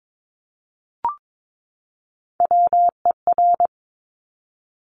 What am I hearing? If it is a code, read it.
Morse code, heard as a keyed tone, about 22 wpm, 700 Hz. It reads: WER
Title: Top Three Letter Combinations in Words 22wpm